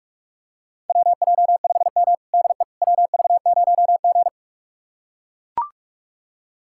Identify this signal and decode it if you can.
Morse code, heard as a keyed tone, about 44 wpm, 700 Hz. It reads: WJ5M DE WV0G
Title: USA Callsign DE Callsign - 4 Characters - 45wpm